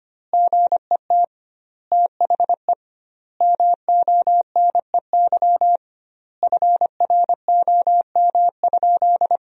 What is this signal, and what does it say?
Morse code, heard as a keyed tone, about 25 wpm, 700 Hz. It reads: GET THE MONEY FROM?
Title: 2x Word-Spacing - Sentences from Top 400 Words - 25wpm